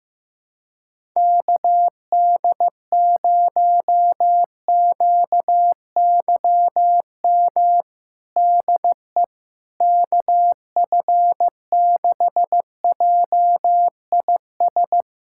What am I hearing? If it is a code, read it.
Morse code, heard as a keyed tone, about 15 wpm, 700 Hz. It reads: KD0QYM DE KF6JIS
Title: USA Callsign DE Callsign - 6 Characters - 15wpm